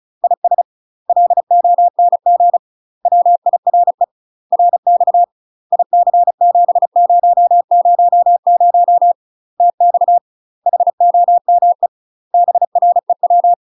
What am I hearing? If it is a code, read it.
Morse code, heard as a keyed tone, about 35 wpm, 700 Hz. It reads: IS LONG WIRE RX IC7000 TX HOME BREW
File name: UK MASTER QSOs TRAINING/35WPM